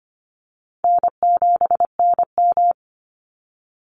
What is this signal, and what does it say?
Morse code, heard as a keyed tone, about 25 wpm, 700 Hz. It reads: N7NM
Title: Call Signs - 25wpm